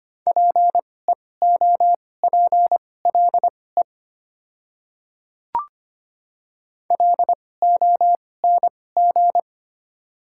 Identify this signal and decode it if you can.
Morse code, heard as a keyed tone, about 25 wpm, spaced as for 18 wpm, 700 Hz. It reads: PEOPLE LONG
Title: 2x Character Spacing - Top 100 Words - 25wpm